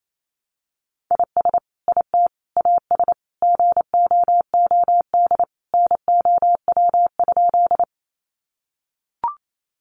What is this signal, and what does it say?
Morse code, heard as a keyed tone, about 28 wpm, 700 Hz. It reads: IS IT AS GOOD NOW?
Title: Sentences from Top 200 Words 28wpm